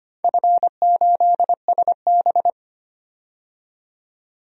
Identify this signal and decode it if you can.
Morse code, heard as a keyed tone, about 25 wpm, 700 Hz. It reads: F8SB
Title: DXCC-100 Call Signs - 4 Characters - 25wpm